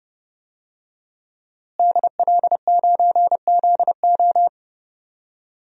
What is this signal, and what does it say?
Morse code, heard as a keyed tone, about 30 wpm, 700 Hz. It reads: DL9ZO
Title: DXCC-100 Call Signs - 5 Characters - Repeated 2x - 30wpm